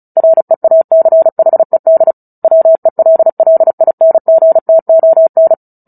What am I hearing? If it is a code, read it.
Morse code, heard as a keyed tone, about 34 wpm, 650 Hz. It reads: REACHED WELLINGTON
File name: Book Two / 09-Chapter 9 - 35 WPM